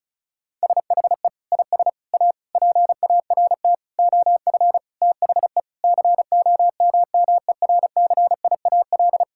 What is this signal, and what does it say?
Morse code, heard as a keyed tone, about 36 wpm, 700 Hz. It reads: SHE IS A PART OF THE COMMERCIAL
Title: Sentences from Top 1000 Words - Review Part 5 - 35wpm